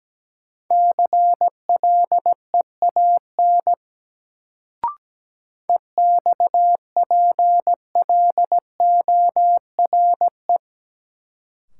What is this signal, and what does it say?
Morse code, heard as a keyed tone, about 17 wpm, 700 Hz. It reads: CLEAN EXPLORE
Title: Sentences from Top 1100 Words - New Word Drill Alt - 17wpm